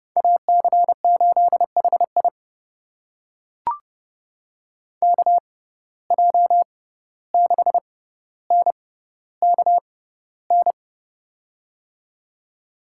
Morse code, heard as a keyed tone, about 31 wpm, spaced as for 15 wpm, 700 Hz. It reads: AC8HI KJ6NKN
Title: USA Call Signs - 6x Character Spacing - 30wpm